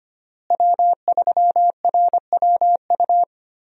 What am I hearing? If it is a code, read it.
Morse code, heard as a keyed tone, about 25 wpm, 700 Hz. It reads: W3RWU
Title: USA Call Signs - 5 Characters - Repeated 3x - 25wpm